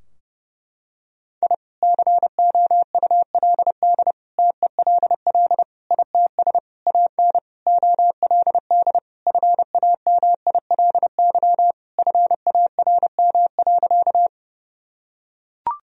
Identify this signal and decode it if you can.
Morse code, heard as a keyed tone, about 30 wpm, 700 Hz. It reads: I COULD TELL ITS AN OLD FAMILY FARM.
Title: Sentences from Top 1000 Words - Review Part 2 - 30wpm